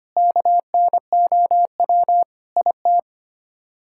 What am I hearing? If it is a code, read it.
Morse code, heard as a keyed tone, about 25 wpm, 700 Hz. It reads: KNOW IT
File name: Sets of 9 Words - 25wpm